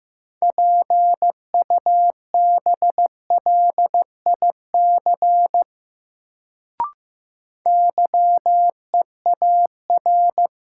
Morse code, heard as a keyed tone, about 15 wpm, 700 Hz. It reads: PUBLIC YEAR
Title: Sets of 2 Words - New Word Drill - 15wpm